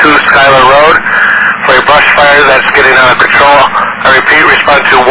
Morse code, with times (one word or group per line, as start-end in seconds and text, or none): none